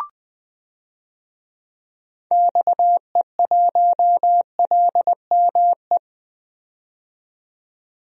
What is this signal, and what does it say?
Morse code, heard as a keyed tone, about 20 wpm, 700 Hz. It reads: XE1LME
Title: DXCC-100 Call Signs - 6 Characters - 20wpm